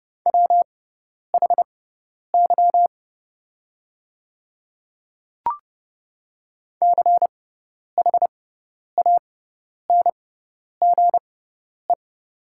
Morse code, heard as a keyed tone, about 30 wpm, spaced as for 10 wpm, 700 Hz. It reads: WHY CHANGE
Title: 6x Character Spacing - Top 200 Words - 30wpm